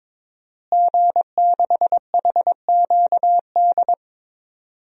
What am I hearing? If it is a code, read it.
Morse code, heard as a keyed tone, about 22 wpm, 700 Hz. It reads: G6HQD
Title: DXCC-100 Call Signs - 5 Characters - Repeated 3x - 22wpm